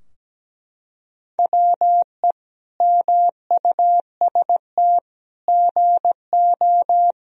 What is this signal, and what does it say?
Morse code, heard as a keyed tone, about 17 wpm, 700 Hz. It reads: WE MUST GO
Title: Sentences from Top 1000 Words - Review Part 3 - 17wpm